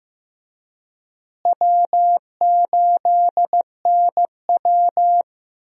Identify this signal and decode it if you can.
Morse code, heard as a keyed tone, about 15 wpm, 700 Hz. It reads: W8NW
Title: USA Call Signs - 4 Characters - Repeated 3x - 15wpm